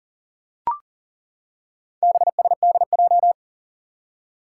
Morse code, heard as a keyed tone, about 38 wpm, 700 Hz. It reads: BSDJ
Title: Four Challenging Letter-Number Combinations 40wpm